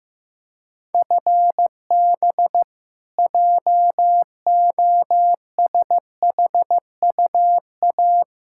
Morse code, heard as a keyed tone, about 15 wpm, 700 Hz. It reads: FB JOSHUA
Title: K1USN SST QSO Practice 15wpm